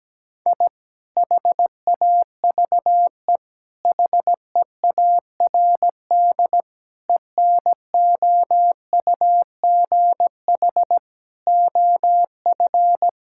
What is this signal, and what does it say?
Morse code, heard as a keyed tone, about 17 wpm, 700 Hz. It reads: I HAVE HEARD ENOUGH OF
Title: Sentences from Top 1000 Words - Review Part 6 - 17wpm